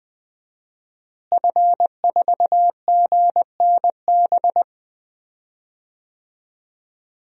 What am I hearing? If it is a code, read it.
Morse code, heard as a keyed tone, about 20 wpm, 700 Hz. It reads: F4GNB
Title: DXCC-100 Call Signs - 5 Characters - Repeated 2x - 20wpm